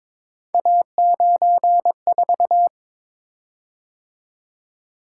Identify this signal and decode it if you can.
Morse code, heard as a keyed tone, about 22 wpm, 700 Hz. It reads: A94
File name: practice / 22wpm / Three Letter-Number Combinations - 22wpm